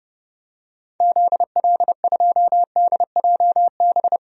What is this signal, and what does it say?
Morse code, heard as a keyed tone, about 31 wpm, 700 Hz. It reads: ZL2DJB
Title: DXCC-100 Call Signs - 6 Characters - 30wpm